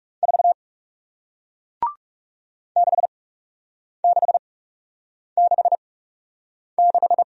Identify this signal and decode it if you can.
Morse code, heard as a keyed tone, about 40 wpm, spaced as for 8 wpm, 700 Hz. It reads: 4 6666